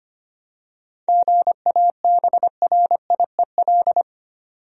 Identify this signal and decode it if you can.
Morse code, heard as a keyed tone, about 25 wpm, 700 Hz. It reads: GABRIEL
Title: USA Common Names - 25wpm